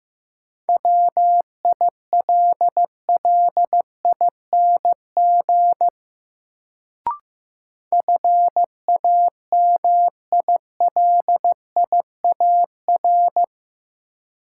Morse code, heard as a keyed tone, about 15 wpm, 700 Hz. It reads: WILLING FAMILIAR
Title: Sentences from Top 1000 Words - New Word Drill - 15wpm